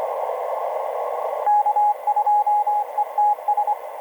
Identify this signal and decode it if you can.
Morse code, heard as a keyed tone, about 24 wpm, 830 Hz, with radio noise. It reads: K2ETS
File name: n0tg-cy0 ve1mc etc 30m cw 07mar2011 2034